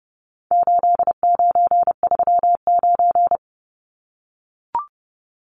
Morse code, heard as a keyed tone, about 30 wpm, 700 Hz. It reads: 8939
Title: Numbers - 4-Digit 30wpm